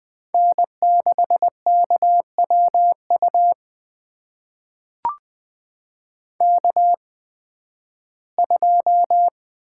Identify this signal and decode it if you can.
Morse code, heard as a keyed tone, about 20 wpm, spaced as for 13 wpm, 700 Hz. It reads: N6KWU K2
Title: USA Call Signs - 5 Characters - 8x Character Spacing - 20wpm